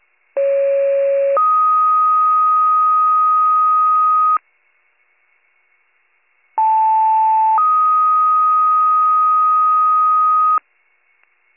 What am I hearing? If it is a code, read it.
Two-tone page, heard as a one-second tone, then a three-second tone, about 550 then 1250 Hz, about 880 then 1250 Hz. It is heard from 0.4 to 10.6 s.